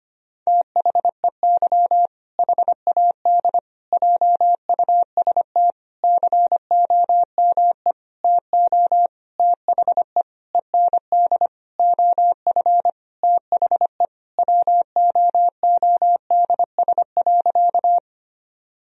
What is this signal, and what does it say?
Morse code, heard as a keyed tone, about 25 wpm, 700 Hz. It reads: THEY HAD JUST COME TO THE END OF THE WOODS.
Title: Sentences from Top 1000 Words - Review Part 2 - 25wpm